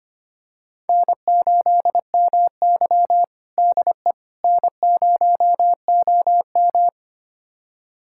Morse code, heard as a keyed tone, about 25 wpm, 700 Hz. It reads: N8MY DE N0OM
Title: USA Callsign DE Callsign - 4 Characters - 25wpm